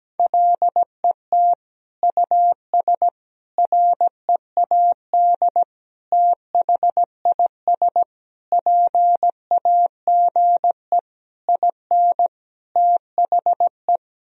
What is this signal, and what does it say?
Morse code, heard as a keyed tone, about 17 wpm, 700 Hz. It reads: LET US READ THIS PAGE IN THE
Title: Sentences from Top 1000 Words - Review Part 1 - 17wpm